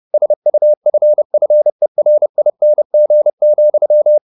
Morse code, heard as a keyed tone, about 30 wpm, 600 Hz. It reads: SUFFERING,
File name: the old man and the sea / the old man and the sea 1 30wpm